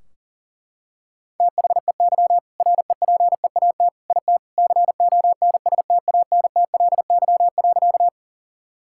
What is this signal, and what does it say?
Morse code, heard as a keyed tone, about 40 wpm, 700 Hz. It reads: THEY REPEAT IT CONSTANTLY.
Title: Sentences from Top 1600 Words 40wpm